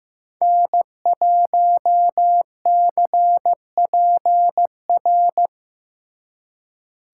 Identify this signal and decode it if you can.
Morse code, heard as a keyed tone, about 18 wpm, 700 Hz. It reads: N1CPR
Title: Speed Racing - USA 1x3 Call Signs 15wpm